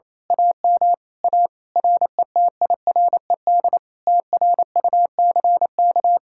Morse code, heard as a keyed tone, about 27 wpm, 700 Hz. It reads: AM A RETIRED TRUCK
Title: Speed-Racing Common QSO Phrases Encore - 28wpm